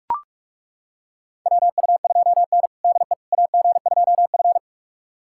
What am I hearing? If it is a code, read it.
Morse code, heard as a keyed tone, about 41 wpm, 700 Hz. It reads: WU2N DE AG2F
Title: USA Callsign DE Callsign - 4 Characters - 45wpm